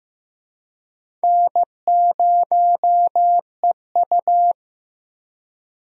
Morse code, heard as a keyed tone, about 16 wpm, 700 Hz. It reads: N0EU
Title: USA Call Signs - 4 Characters - Repeated 3x - 15wpm